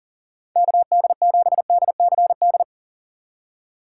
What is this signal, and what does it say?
Morse code, heard as a keyed tone, about 40 wpm, 700 Hz. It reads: KD7DCD
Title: USA Call Signs - 3x Character Spacing - 40wpm